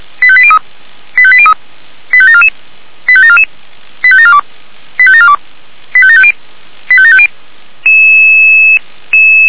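Five-tone selective call (ZVEI), heard as quick runs of five tones, tones of about 70 ms. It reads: 85803 85803 85744 85744 85742 85742 85750 85750
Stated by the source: ZVEI five-tone call received by SDR.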